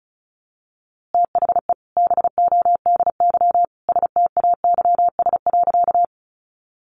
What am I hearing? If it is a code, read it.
Morse code, heard as a keyed tone, about 34 wpm, 700 Hz. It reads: THE BODY STAYS.
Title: A Sound Of Thunder 35wpm